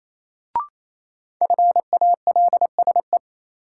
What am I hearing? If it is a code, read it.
Morse code, heard as a keyed tone, about 28 wpm, 700 Hz. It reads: FALSE
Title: Sentences from Top 1600 Words - New Word Drill Alt - 28wpm